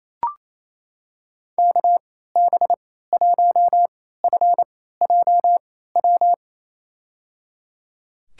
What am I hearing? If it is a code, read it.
Morse code, heard as a keyed tone, about 28 wpm, spaced as for 16 wpm, 700 Hz. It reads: KB1FJW
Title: USA Call Signs - 6 Characters - 3x Character Spacing - 28wpm